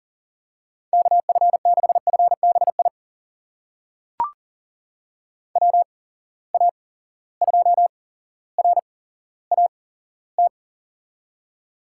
Morse code, heard as a keyed tone, about 40 wpm, spaced as for 17 wpm, 700 Hz. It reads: KF6FBI WA2RAT
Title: USA Call Signs - 6 Characters - 8x Character Spacing - 40wpm